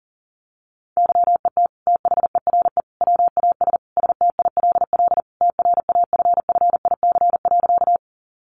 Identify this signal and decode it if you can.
Morse code, heard as a keyed tone, about 40 wpm, 700 Hz. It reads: YET THERE WAS STILL TRAFFIC.
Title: Sentences from Top 1300 Words 40wpm